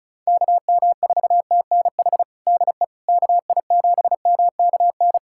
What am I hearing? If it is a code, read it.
Morse code, heard as a keyed tone, about 34 wpm, 700 Hz. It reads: KM4TNH DE KI7MKN
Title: USA Callsign DE Callsign - 6 Characters - 35wpm